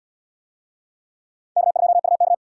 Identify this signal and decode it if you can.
Morse code, heard as a keyed tone, about 77 wpm, 700 Hz. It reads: K2UK